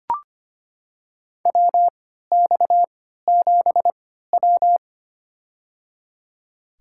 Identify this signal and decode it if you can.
Morse code, heard as a keyed tone, about 25 wpm, spaced as for 14 wpm, 700 Hz. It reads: WX7W
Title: USA Call Signs - 4 Characters - 3x Character Spacing - 25wpm